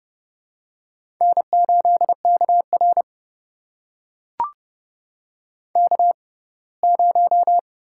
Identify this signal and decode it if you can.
Morse code, heard as a keyed tone, about 30 wpm, spaced as for 20 wpm, 700 Hz. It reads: N8KR K0
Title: USA Call Signs - 4 Characters - 6x Character Spacing - 30wpm